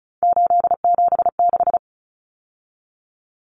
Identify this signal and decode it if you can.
Morse code, heard as a keyed tone, about 34 wpm, 700 Hz. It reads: 876
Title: Numbers - 3-Digit 35wpm